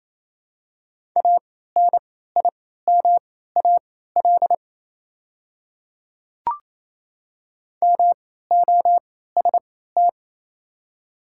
Morse code, heard as a keyed tone, about 28 wpm, spaced as for 16 wpm, 700 Hz. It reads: ANIMAL MOST
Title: 3x Character Spacing - Top 500 Words - 28wpm